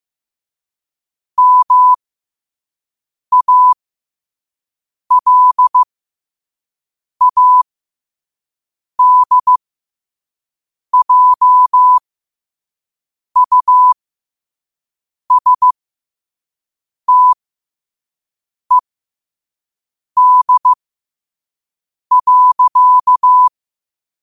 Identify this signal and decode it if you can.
Morse code, heard as a keyed tone, about 15 wpm, spaced as for 5 wpm, 1000 Hz. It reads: MALADJUSTED.
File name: qotdfiles / 5wpm